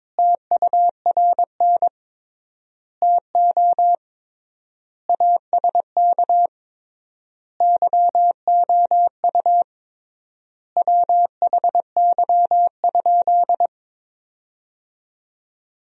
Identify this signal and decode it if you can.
Morse code, heard as a keyed tone, about 21 wpm, 700 Hz. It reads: TURN TO ASK YOU WHY?
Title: 3x Word-Spacing - Sentences from Top 200 Words - 22wpm